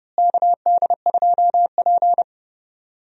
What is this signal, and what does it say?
Morse code, heard as a keyed tone, about 32 wpm, 700 Hz. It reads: KD2P
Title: USA Call Signs - 4 Characters - Repeated 3x - 30wpm